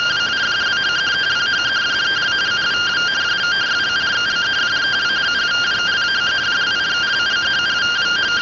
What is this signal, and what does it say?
Two-tone teleprinter signal, about 170 Hz shift.